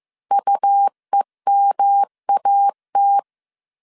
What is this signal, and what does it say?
Morse code, heard as a keyed tone, about 15 wpm, 800 Hz. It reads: UEMAT